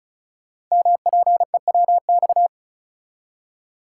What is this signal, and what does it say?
Morse code, heard as a keyed tone, about 35 wpm, 700 Hz. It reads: MPEWX
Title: Code Groups - 5 Letters - 35wpm